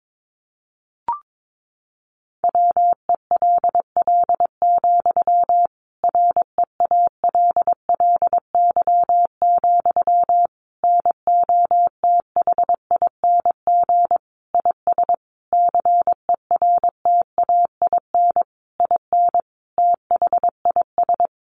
Morse code, heard as a keyed tone, about 22 wpm, 700 Hz. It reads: WELL, REALLY, NOTHING IS CERTAIN IN THIS
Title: Sentences from Top 600 Words 22wpm